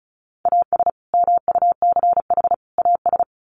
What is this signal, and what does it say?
Morse code, heard as a keyed tone, about 34 wpm, 700 Hz. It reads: AS MUCH AS
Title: Sets of 3 Words 35wpm